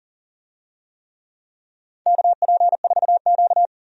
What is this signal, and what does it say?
Morse code, heard as a keyed tone, about 38 wpm, 700 Hz. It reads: KP4Q
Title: DXCC-100 Call Signs - 4 Characters - 40wpm